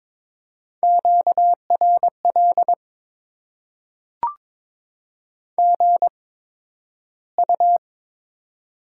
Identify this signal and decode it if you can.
Morse code, heard as a keyed tone, about 22 wpm, spaced as for 12 wpm, 700 Hz. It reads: QRL GU
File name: Top 20 QSO Elements - 8x Character Spacing - 22wpm